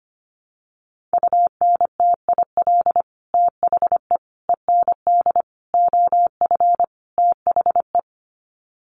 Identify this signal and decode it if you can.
Morse code, heard as a keyed tone, about 25 wpm, 700 Hz. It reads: UNTIL THE END OF THE
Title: Sets of 5 Words Encore 25wpm